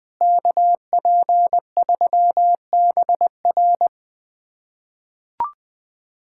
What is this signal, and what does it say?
Morse code, heard as a keyed tone, about 21 wpm, 700 Hz. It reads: KP3BR
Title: USA Callsign DE Callsign - 5 Characters - 20wpm